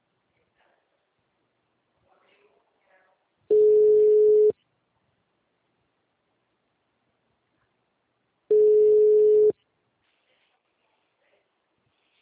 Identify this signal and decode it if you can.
Ringback tone (Europe, 425 Hz).